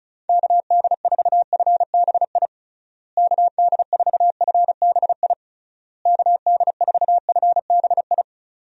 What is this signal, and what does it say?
Morse code, heard as a keyed tone, about 34 wpm, 700 Hz. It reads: KD4FBI KD4FBI KD4FBI
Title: USA Call Signs - 6 Characters - Repeated 3x - 35wpm